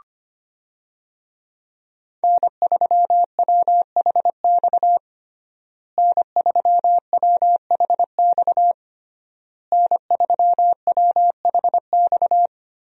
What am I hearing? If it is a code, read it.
Morse code, heard as a keyed tone, about 25 wpm, 700 Hz. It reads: N3WHX N3WHX N3WHX
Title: USA Call Signs - 5 Characters - Repeated 3x - 25wpm